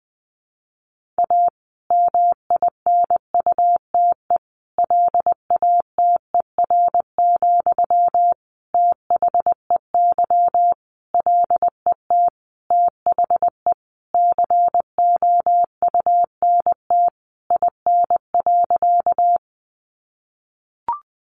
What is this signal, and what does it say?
Morse code, heard as a keyed tone, about 20 wpm, 700 Hz. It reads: A MINUTE LATER, THEY LET THE COUNT IN.
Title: Sentences from Top 1500 Words 20wpm